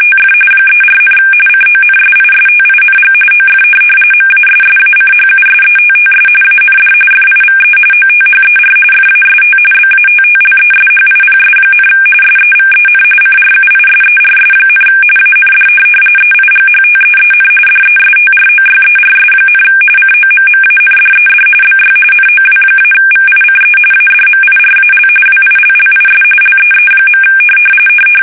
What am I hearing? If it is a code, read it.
RTTY, about 45.45 baud. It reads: TF 34, CONSISTING OF MASSACHUSETTS AND FIVE OTHER FAST BATTLESHIPS, SEVEN CRUISERS, AND EIGHTEEN DESTROYERS, COMMANDED BY NOW-VICE ADMIRAL WILLIS LEE.
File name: fsk850 / USS-Massachusetts